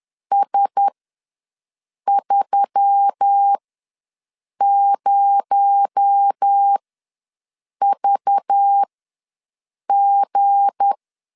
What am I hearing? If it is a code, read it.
Morse code, heard as a keyed tone, about 10 wpm, spaced as for 6 wpm, 800 Hz. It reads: S30VG
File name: LEZIONI / 7 GF9Z / 7 GF9Z ws 10wpm 3919